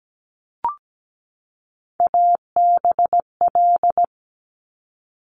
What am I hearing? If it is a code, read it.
Morse code, heard as a keyed tone, about 17 wpm, 700 Hz. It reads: ABL
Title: Top Three Letter Combinations in Words 17wpm